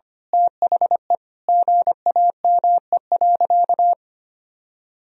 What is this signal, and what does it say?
Morse code, heard as a keyed tone, about 25 wpm, 700 Hz. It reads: THE GAME.
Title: Sentences from Top 1000 Words - Review Part 3 - 25wpm